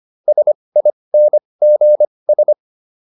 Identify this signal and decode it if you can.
Morse code, heard as a keyed tone, about 25 wpm, spaced as for 20 wpm, 600 Hz. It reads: SINGS